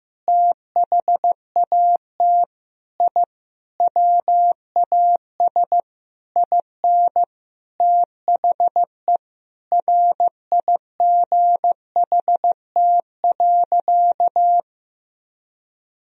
Morse code, heard as a keyed tone, about 15 wpm, 700 Hz. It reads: THAT I WAS IN THE RIGHT.
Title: Sentences from Top 1000 Words - Review Part 1 - 15wpm